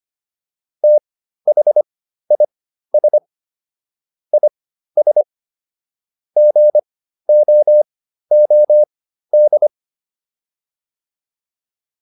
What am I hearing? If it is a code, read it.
Morse code, heard as a keyed tone, about 25 wpm, spaced as for 13 wpm, 600 Hz. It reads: THIS IS GOOD